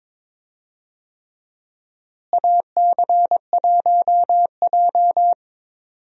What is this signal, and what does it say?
Morse code, heard as a keyed tone, about 22 wpm, 700 Hz. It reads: AC1J
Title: USA Call Signs - 4 Characters - Repeated 3x - 22wpm